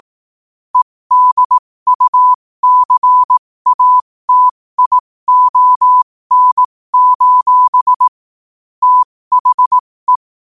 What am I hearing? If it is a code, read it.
Morse code, heard as a keyed tone, about 18 wpm, spaced as for 15 wpm, 1000 Hz. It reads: EDUCATION: THE
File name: qotdfiles / 15wpm